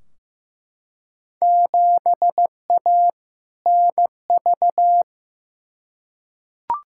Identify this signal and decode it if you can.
Morse code, heard as a keyed tone, about 15 wpm, 700 Hz. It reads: 7A NV
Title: Field Day Exchanges Encore - 15wpm